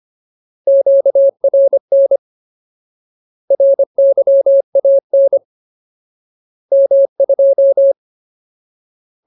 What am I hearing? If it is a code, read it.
Morse code, heard as a keyed tone, about 25 wpm, 550 Hz. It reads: QRN RYAN M2
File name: Ultimate Word List / 25WPM Ultimate